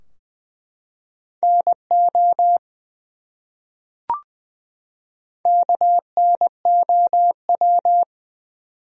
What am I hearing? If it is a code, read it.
Morse code, heard as a keyed tone, about 20 wpm, 700 Hz. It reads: NO KNOW
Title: Top 100 Words 20wpm